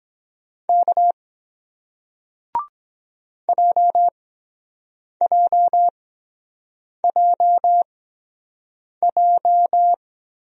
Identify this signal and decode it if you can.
Morse code, heard as a keyed tone, about 23 wpm, spaced as for 7 wpm, 700 Hz. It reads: K JJJJ